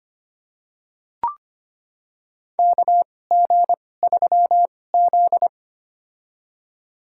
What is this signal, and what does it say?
Morse code, heard as a keyed tone, about 25 wpm, spaced as for 18 wpm, 700 Hz. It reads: KG3Z
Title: USA Call Signs - 2x Character Spacing - 25wpm